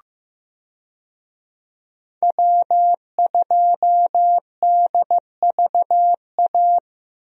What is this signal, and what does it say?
Morse code, heard as a keyed tone, about 15 wpm, 700 Hz. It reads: W2DVA
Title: USA Call Signs - 5 Characters - 15wpm